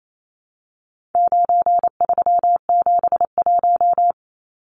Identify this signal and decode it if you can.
Morse code, heard as a keyed tone, about 28 wpm, 700 Hz. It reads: 9371
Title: Numbers - 4-Digit 28wpm